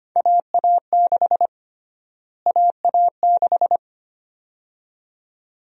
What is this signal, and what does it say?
Morse code, heard as a keyed tone, about 25 wpm, 700 Hz. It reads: AA6 AA6
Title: USA Call Sign Prefixes - Repeated 2x - 25wpm